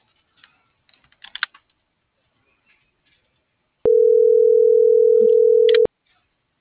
Ringback tone (North America), 440 + 480 Hz.
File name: CENTRAL MULTIMODAL / out viridiana.juarez